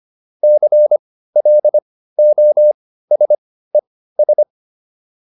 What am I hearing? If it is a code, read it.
Morse code, heard as a keyed tone, about 25 wpm, spaced as for 15 wpm, 600 Hz. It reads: CLOSES
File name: ES6 15